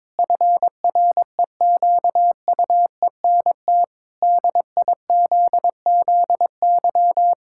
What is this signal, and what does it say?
Morse code, heard as a keyed tone, about 21 wpm, 700 Hz. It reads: FREQUENT DIZZY
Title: Pangram Challenge v2 - 22wpm